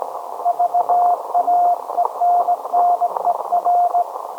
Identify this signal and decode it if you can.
Morse code, heard as a keyed tone, about 18 wpm, 710 Hz, with radio noise. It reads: VAENNER
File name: storage / message Grimeton SAQ